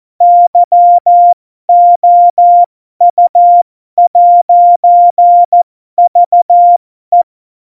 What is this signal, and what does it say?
Morse code, heard as a keyed tone, about 14 wpm, spaced as for 12 wpm, 700 Hz. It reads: YOU'VE
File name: Morsekurs / Witze